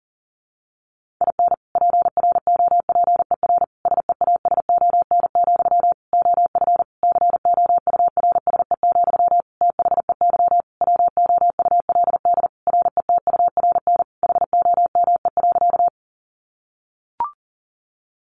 Morse code, heard as a keyed tone, about 40 wpm, 700 Hz. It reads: IN PROPER SEASON, OF COURSE, THEY WOULD RETURN HOME.
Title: Sentences from Top 1400 Words 40wpm